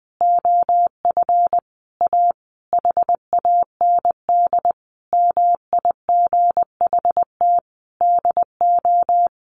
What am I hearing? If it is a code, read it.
Morse code, heard as a keyed tone, about 19 wpm, 700 Hz. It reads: OF A HAND MIGHT DO
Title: A Sound Of Thunder 20wpm